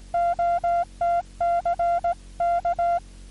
Morse code, heard as a keyed tone, about 18 wpm, 1380 Hz, with noise. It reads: OTC K
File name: 6M audio / W1JJM AU GI0OTC 27Dec2000 1618z